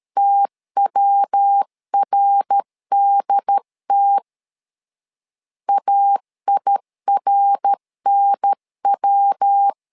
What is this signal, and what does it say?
Morse code, heard as a keyed tone, about 13 wpm, 800 Hz. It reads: TWRDT AIRNW
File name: LEZIONI / 3 MWRO / 3 MWR 12wpm 1711